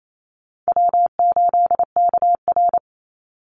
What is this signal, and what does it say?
Morse code, heard as a keyed tone, about 28 wpm, 690 Hz. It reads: W8KR
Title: Call Signs Repeated 3x - 28wpm